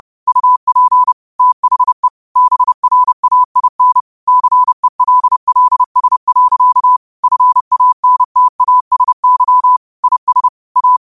Morse code, heard as a keyed tone, about 30 wpm, 1000 Hz. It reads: UP THE BRAIN CELLS. FANTASY IS A